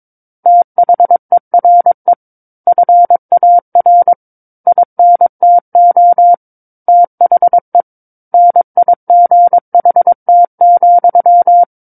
Morse code, heard as a keyed tone, about 22 wpm, 700 Hz. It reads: THERE FAR INTO THE NIGHT,
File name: Book One / 13-Chapter 13 - 22 WPM